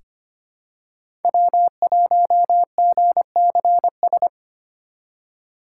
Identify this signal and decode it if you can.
Morse code, heard as a keyed tone, about 25 wpm, 700 Hz. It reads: W1GCS
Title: K1USN SST QSO Practice - 25wpm